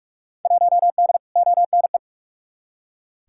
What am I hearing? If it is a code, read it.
Morse code, heard as a keyed tone, about 42 wpm, 700 Hz. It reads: 1D ONE
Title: Field Day Exchanges - 45wpm